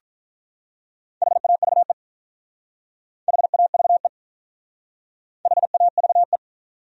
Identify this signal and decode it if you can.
Morse code, heard as a keyed tone, about 45 wpm, 700 Hz. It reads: HAVE HAVE HAVE